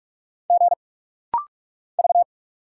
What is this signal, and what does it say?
Morse code, heard as a keyed tone, about 45 wpm, 700 Hz. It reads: G V
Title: Single Letter-Number - Rapid-Fire 45wpm